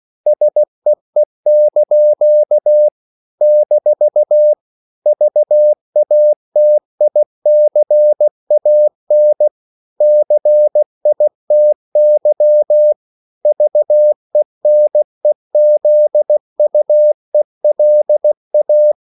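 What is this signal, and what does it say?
Morse code, heard as a keyed tone, about 16 wpm, 600 Hz. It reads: SEE) - VATICAN CITY VENEZUELA
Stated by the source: CW practice of national capitals at 16 words per minute, from ON6ZQ.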